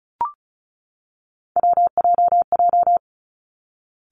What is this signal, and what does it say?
Morse code, heard as a keyed tone, about 36 wpm, 700 Hz. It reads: WJJ
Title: Three Challenging Letter-Number Combinations 35wpm